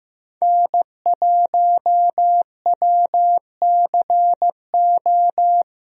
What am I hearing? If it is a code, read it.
Morse code, heard as a keyed tone, about 15 wpm, 700 Hz. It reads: N1WCO
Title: USA Call Signs - 5 Characters - Repeated 3x - 15wpm